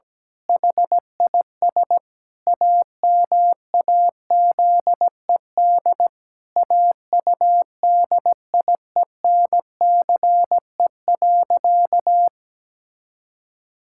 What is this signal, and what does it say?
Morse code, heard as a keyed tone, about 17 wpm, 700 Hz. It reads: HIS AMAZED AUDIENCE.
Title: Pangram Challenge v2 - 17wpm